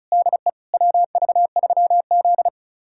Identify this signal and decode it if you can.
Morse code, heard as a keyed tone, about 36 wpm, 700 Hz. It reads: DE WV3Z
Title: USA Callsign DE Callsign - 4 Characters - 35wpm